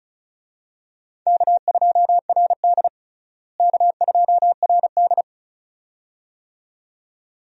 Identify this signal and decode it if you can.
Morse code, heard as a keyed tone, about 36 wpm, 700 Hz. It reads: K2RD K2RD
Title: USA Call Signs - 4 Characters - Repeated 2x - 35wpm